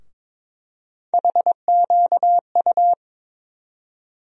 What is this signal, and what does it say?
Morse code, heard as a keyed tone, about 22 wpm, 700 Hz. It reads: HQU
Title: USA Call Sign Postfixes - 3 Letters 22wpm